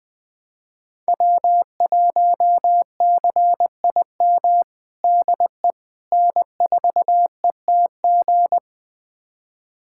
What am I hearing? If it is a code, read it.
Morse code, heard as a keyed tone, about 20 wpm, 700 Hz. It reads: W1CIM DE N4ETG
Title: USA Callsign DE Callsign - 5 Characters - 20wpm